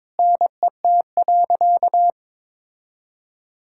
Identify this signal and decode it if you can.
Morse code, heard as a keyed tone, about 22 wpm, 700 Hz. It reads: NET.
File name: Sentences with 9-Word Phrases - 22wpm